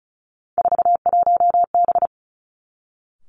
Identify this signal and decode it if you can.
Morse code, heard as a keyed tone, about 36 wpm, 700 Hz. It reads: 41B
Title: Three Challenging Letter-Number Combinations 35wpm